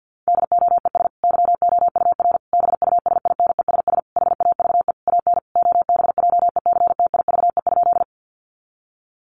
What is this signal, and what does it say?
Morse code, heard as a keyed tone, about 50 wpm, 700 Hz. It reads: DOES YOUR BUSINESS HAVE AN OBJECTIVE?
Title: Sentences from Top 1200 Words 50wpm